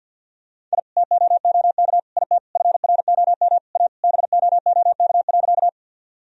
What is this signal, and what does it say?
Morse code, heard as a keyed tone, about 50 wpm, 700 Hz. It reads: I TOOK IT FROM A BOOK.
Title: Sentences from Top 1000 Words - Review Part 1 - 50wpm